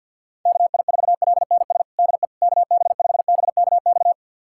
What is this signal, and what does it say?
Morse code, heard as a keyed tone, about 52 wpm, 700 Hz. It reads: KI4LNS DE KB5BKX
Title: USA Callsign DE Callsign 50wpm